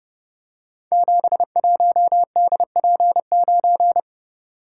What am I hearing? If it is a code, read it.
Morse code, heard as a keyed tone, about 30 wpm, 700 Hz. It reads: 71DP9
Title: Code Groups - 5 Letter:Numbers - 30wpm